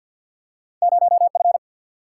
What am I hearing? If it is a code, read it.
Morse code, heard as a keyed tone, about 47 wpm, 700 Hz. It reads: 0F
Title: Single-Letter Single-Number 50wpm